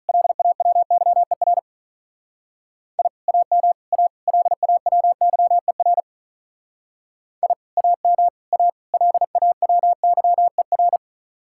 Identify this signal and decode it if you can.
Morse code, heard as a keyed tone, about 43 wpm, 700 Hz. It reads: LAWYER I AM A LAWYER I AM A LAWYER
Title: Speed-Racing Common QSO Phrases - 35wpm